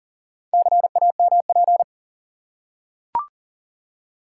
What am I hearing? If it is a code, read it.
Morse code, heard as a keyed tone, about 40 wpm, 700 Hz. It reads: CAMP
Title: Four Letter Words - 40wpm